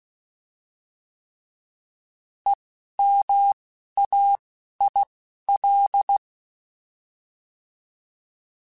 Morse code, heard as a keyed tone, about 16 wpm, spaced as for 11 wpm, 800 Hz. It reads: EMAIL
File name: Alphabetical / Alphabetical mp3s / Es @10